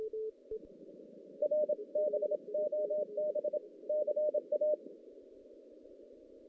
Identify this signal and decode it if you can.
Morse code, heard as a keyed tone, about 27 wpm, 600 Hz. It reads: R BOB CA
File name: 2016 JAN CW NAQP / 2016 Jan naqp k5ot - 0-5